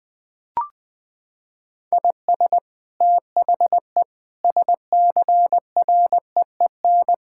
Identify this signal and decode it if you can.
Morse code, heard as a keyed tone, about 20 wpm, 700 Hz. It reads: IS THE SCREEN
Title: Sentences from Top 1000 Words - Review Part 6 - 20wpm